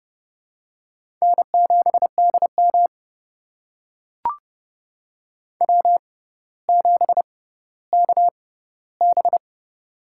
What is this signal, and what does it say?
Morse code, heard as a keyed tone, about 30 wpm, spaced as for 15 wpm, 700 Hz. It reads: N7DM W7KB
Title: USA Call Signs - 4 Characters - 6x Character Spacing - 30wpm